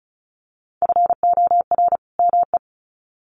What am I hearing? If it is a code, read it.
Morse code, heard as a keyed tone, about 34 wpm, 700 Hz. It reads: FOR ME
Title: Sets of 2 Words 35wpm